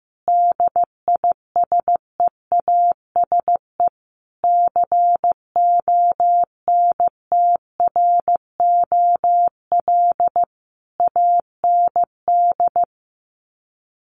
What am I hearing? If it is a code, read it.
Morse code, heard as a keyed tone, about 14 wpm, 700 Hz. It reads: DISEASE CONTROL AND
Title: Sets of 5 Words Encore2 15wpm